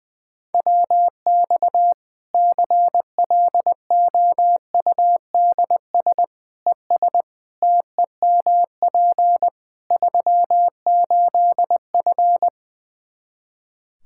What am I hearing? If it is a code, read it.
Morse code, heard as a keyed tone, about 20 wpm, 700 Hz. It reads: WX CLOUDS ES TEMP 38F
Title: Practice QSO - Exchange 2 - Info - 20wpm